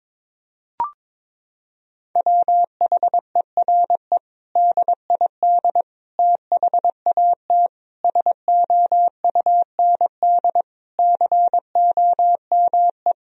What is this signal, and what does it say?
Morse code, heard as a keyed tone, about 21 wpm, 700 Hz. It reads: WHERE DID THAT SOUND COME
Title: Sentences from Top 1000 Words - Review Part 1 - 22wpm